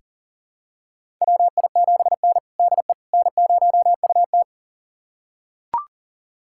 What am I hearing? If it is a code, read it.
Morse code, heard as a keyed tone, about 40 wpm, 690 Hz. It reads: WI7N DE N0UT
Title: USA Callsign DE Callsign - 4 Characters - 40wpm